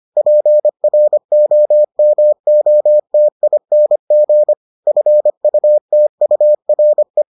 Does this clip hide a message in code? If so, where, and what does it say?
Yes: Morse code from 0.2 to 7.2 s, heard as a keyed tone, about 25 wpm, 600 Hz. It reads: PROMOTING FUTURE